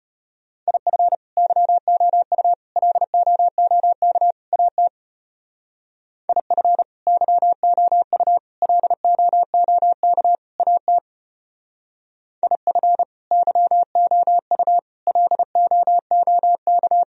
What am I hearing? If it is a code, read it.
Morse code, heard as a keyed tone, about 32 wpm, 700 Hz. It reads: IF YOU LOOK AT IF YOU LOOK AT IF YOU LOOK